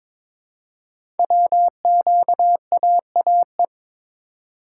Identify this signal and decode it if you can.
Morse code, heard as a keyed tone, about 22 wpm, 700 Hz. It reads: WQAAE
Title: Code Groups - 5 Letter:Numbers - 22wpm